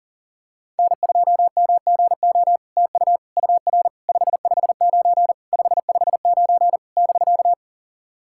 Morse code, heard as a keyed tone, about 40 wpm, 700 Hz. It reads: N2MGO TU UR 559 559 <BK>
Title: SOTA QSO Practice 40wpm